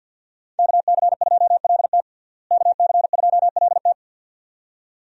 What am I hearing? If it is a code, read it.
Morse code, heard as a keyed tone, about 52 wpm, 700 Hz. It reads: KC2LT KC2LT
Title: USA Call Signs - 5 Characters - Repeated 2x - 50wpm